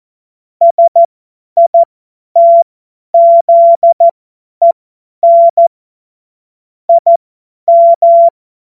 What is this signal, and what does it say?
Morse code, heard as a keyed tone, about 14 wpm, spaced as for 10 wpm, 680 Hz. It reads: SITZEN IM